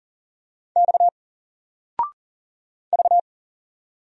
Morse code, heard as a keyed tone, about 40 wpm, 700 Hz. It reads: X V
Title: Single Letter-Number 40wpm